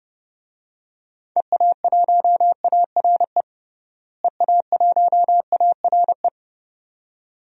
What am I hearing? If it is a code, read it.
Morse code, heard as a keyed tone, about 30 wpm, 700 Hz. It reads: EA1ARE EA1ARE
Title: DXCC-100 Call Signs - 6 Characters - Repeated 2x - 30wpm